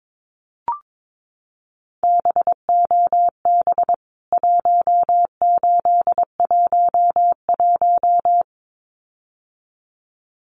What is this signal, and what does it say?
Morse code, heard as a keyed tone, about 22 wpm, 700 Hz. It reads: BOB 1811
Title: CWT Exchange 22wpm